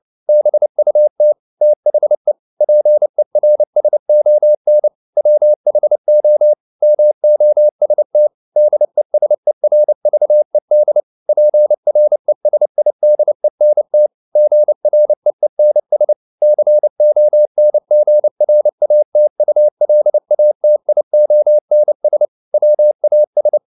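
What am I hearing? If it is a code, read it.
Morse code, heard as a keyed tone, about 29 wpm, 600 Hz. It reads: BUT THE PERSON WHO MOST DESERVED PRESIDENT GREENS CONGRATULATIONS WAS
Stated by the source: Morse code at 29 wpm, sending Ken Follett's novel Never in English, from CW med Gnister.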